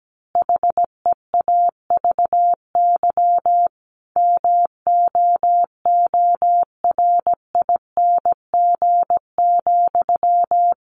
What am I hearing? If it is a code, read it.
Morse code, heard as a keyed tone, about 17 wpm, 700 Hz. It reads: HEAVY MOORING,
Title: A Sound Of Thunder 17wpm